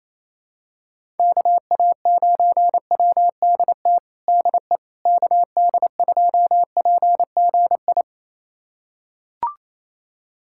Morse code, heard as a keyed tone, about 27 wpm, 700 Hz. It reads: KA9WDT DE KD2PGI
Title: USA Callsign DE Callsign - 6 Characters - 28wpm